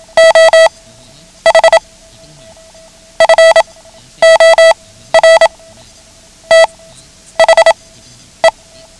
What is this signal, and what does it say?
Morse code, heard as a keyed tone, about 27 wpm, spaced as for 11 wpm, 650 Hz. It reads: OH FOR THE
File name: ham radio / CodeQuick / Code Quick 09